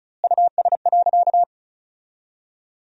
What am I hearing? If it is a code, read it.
Morse code, heard as a keyed tone, about 36 wpm, 700 Hz. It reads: US.
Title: 3x Word-Spacing - Sentences from Top 400 Words - 35wpm